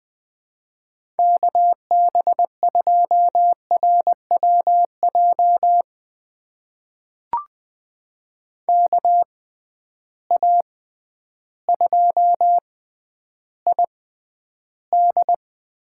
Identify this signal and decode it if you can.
Morse code, heard as a keyed tone, about 20 wpm, spaced as for 11 wpm, 700 Hz. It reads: KB2RWJ KA2ID
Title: USA Call Signs - 6 Characters - 6x Character Spacing - 20wpm